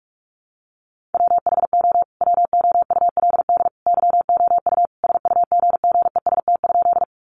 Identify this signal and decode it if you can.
Morse code, heard as a keyed tone, about 44 wpm, 700 Hz. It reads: WHO WOULD YOU SUGGEST?
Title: Sentences from Top 1300 Words 45wpm